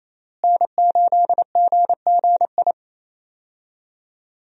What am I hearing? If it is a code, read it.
Morse code, heard as a keyed tone, about 28 wpm, 700 Hz. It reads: N8GGI
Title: USA Call Signs - 5 Characters - Repeated 2x - 28wpm